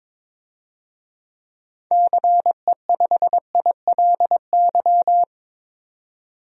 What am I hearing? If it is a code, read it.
Morse code, heard as a keyed tone, about 21 wpm, 700 Hz. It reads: CE5ILY
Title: DXCC-100 Call Signs - 6 Characters - 22wpm